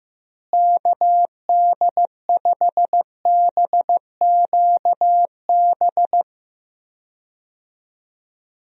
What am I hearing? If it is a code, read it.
Morse code, heard as a keyed tone, about 16 wpm, 700 Hz. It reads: KD5BQB
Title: USA Call Signs - 6 Characters - Repeated 2x - 15wpm